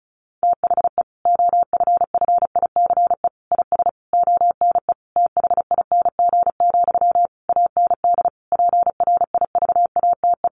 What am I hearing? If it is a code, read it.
Morse code, heard as a keyed tone, about 34 wpm, 700 Hz. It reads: THE OFFICE IS ONE THING, AND PRIVATE
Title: Sentences from Top 1000 Words 35wpm